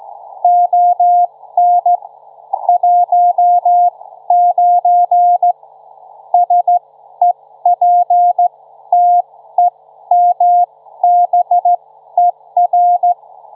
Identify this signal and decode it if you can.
Morse code, heard as a keyed tone, about 15 wpm, spaced as for 12 wpm, 710 Hz. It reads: ON 19 SEPTEMBER
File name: Trivia-ZL2WT